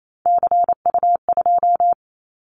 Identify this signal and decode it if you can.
Morse code, heard as a keyed tone, about 28 wpm, 700 Hz. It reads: CU2
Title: Callsign Prefixes for Top 50 DXCC - 28wpm